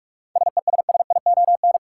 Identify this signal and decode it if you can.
Morse code, heard as a keyed tone, about 46 wpm, 700 Hz. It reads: SESSION
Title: Seven Letter Words - 45wpm